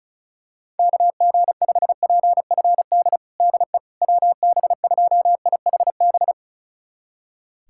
Morse code, heard as a keyed tone, about 36 wpm, 700 Hz. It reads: KG5PFD DE WB2IHB
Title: USA Callsign DE Callsign 35wpm